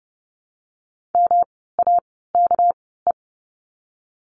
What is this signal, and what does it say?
Morse code, heard as a keyed tone, about 30 wpm, spaced as for 17 wpm, 700 Hz. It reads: MAKE
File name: course / 30:10wpm / 230 Lesson 37f - Letter x - Words 30wpm